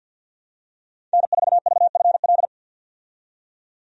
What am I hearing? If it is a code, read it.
Morse code, heard as a keyed tone, about 46 wpm, 700 Hz. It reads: N4VFL
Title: USA Call Signs - 5 Characters - 50wpm